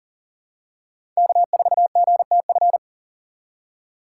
Morse code, heard as a keyed tone, about 40 wpm, 700 Hz. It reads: K4GTF
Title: USA Call Signs - 3x Character Spacing - 40wpm